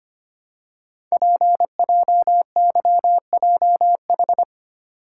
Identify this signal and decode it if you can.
Morse code, heard as a keyed tone, about 25 wpm, 700 Hz. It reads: PJYJH
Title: Code Groups - 5 Letters - 25wpm